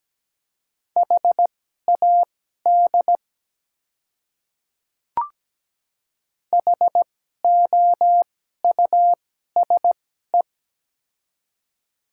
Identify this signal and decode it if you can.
Morse code, heard as a keyed tone, about 17 wpm, spaced as for 12 wpm, 700 Hz. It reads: HAD HOUSE
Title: 2x Character Spacing - Top 200 Words - 17wpm